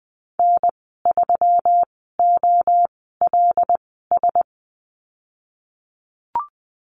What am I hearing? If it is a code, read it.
Morse code, heard as a keyed tone, about 20 wpm, spaced as for 14 wpm, 700 Hz. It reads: N3OLS
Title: 161 Lesson 26g - Letter g - Call Signs 20wpm